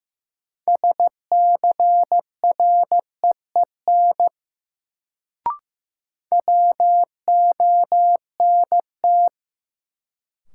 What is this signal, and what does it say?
Morse code, heard as a keyed tone, about 15 wpm, 700 Hz. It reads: SCREEN WONT
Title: Sentences from Top 1000 Words - Review Part 6 - New Word Drill - 15wpm